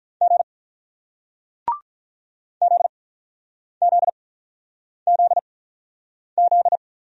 Morse code, heard as a keyed tone, about 47 wpm, spaced as for 9 wpm, 700 Hz. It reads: G ZZZZ